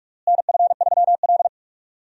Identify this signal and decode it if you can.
Morse code, heard as a keyed tone, about 46 wpm, 700 Hz. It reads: NF3L
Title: USA Call Signs - 4 Characters - Repeated 2x - 45wpm